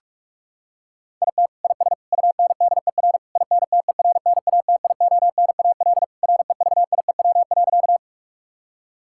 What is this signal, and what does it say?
Morse code, heard as a keyed tone, about 46 wpm, 700 Hz. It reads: IT IS UNDER INTERNATIONAL REVIEW.
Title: Sentences from Top 1000 Words - Review Part 6 - 45wpm